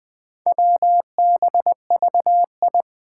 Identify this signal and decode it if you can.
Morse code, heard as a keyed tone, about 20 wpm, 700 Hz. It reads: WBVI
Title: Four Challenging Letter-Number Combinations 20wpm